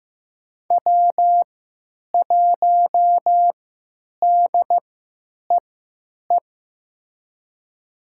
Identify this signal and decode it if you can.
Morse code, heard as a keyed tone, about 15 wpm, spaced as for 9 wpm, 700 Hz. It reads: W1DEE
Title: USA Call Signs - 5 Characters - 3x Character Spacing - 15wpm